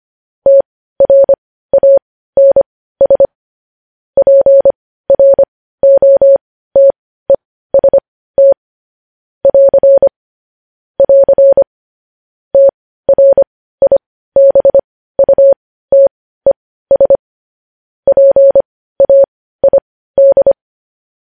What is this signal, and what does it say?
Morse code, heard as a keyed tone, about 25 wpm, spaced as for 15 wpm, 550 Hz. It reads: TRANS PROTEST <AR> <AR> TRIBUTES PAID
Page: News Headlines in Morse Code at 15 WPM